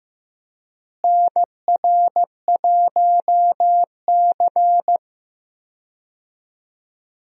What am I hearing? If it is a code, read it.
Morse code, heard as a keyed tone, about 15 wpm, 700 Hz. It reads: NR1C
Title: USA Call Signs - 4 Characters - Repeated 3x - 15wpm